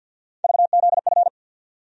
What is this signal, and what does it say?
Morse code, heard as a keyed tone, about 50 wpm, 700 Hz. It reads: VZF
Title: USA Call Sign Postfixes - 3 Letters 50wpm